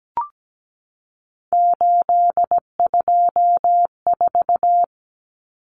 Morse code, heard as a keyed tone, about 17 wpm, 700 Hz. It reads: 824
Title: Common Cut Numbers - 3-Digit 17wpm